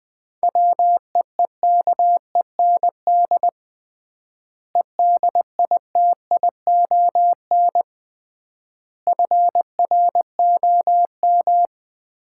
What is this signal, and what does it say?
Morse code, heard as a keyed tone, about 20 wpm, 700 Hz. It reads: WEEKEND EDITION FROM
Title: 3x Word-Spacing - Sets of 5 Words - 20wpm